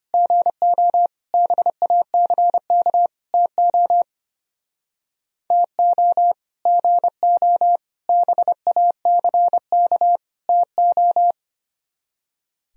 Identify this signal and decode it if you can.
Morse code, heard as a keyed tone, about 32 wpm, 700 Hz. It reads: GO BACK TO TO GO BACK TO